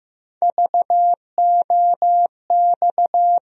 Morse code, heard as a keyed tone, about 15 wpm, 700 Hz. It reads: VOX
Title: USA Call Sign Postfixes - 3 Letters 15wpm